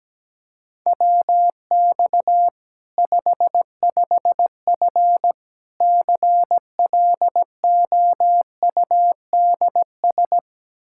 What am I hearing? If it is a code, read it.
Morse code, heard as a keyed tone, about 17 wpm, 700 Hz. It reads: WX 55F CLOUDS
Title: Practice QSO - Exchange 2 - Info - 17wpm